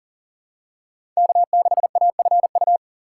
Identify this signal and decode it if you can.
Morse code, heard as a keyed tone, about 40 wpm, 700 Hz. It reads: K6AFU
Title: USA Call Signs - 5 Characters - 3x Character Spacing - 40wpm